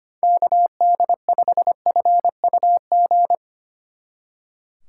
Morse code, heard as a keyed tone, about 32 wpm, 700 Hz. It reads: KD5FUG